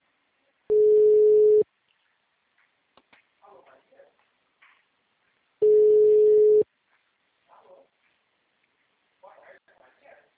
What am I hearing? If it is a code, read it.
Ringback tone (Europe, 425 Hz), with speech.